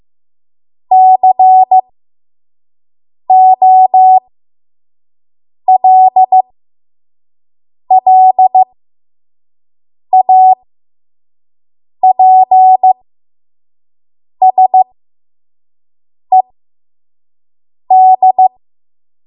Morse code, heard as a keyed tone, about 15 wpm, spaced as for 5 wpm, 750 Hz. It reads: COLLAPSED
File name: Archive / 5 WPM / 05WPM